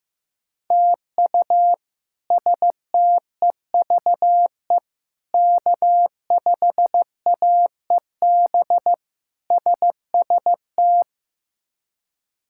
Morse code, heard as a keyed tone, about 15 wpm, 700 Hz. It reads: TU STEVE K5AEB SST
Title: K1USN SST QSO Practice - 15wpm